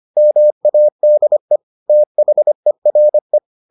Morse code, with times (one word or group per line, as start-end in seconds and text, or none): 0.2-1.6 MADE
1.9-3.4 THERE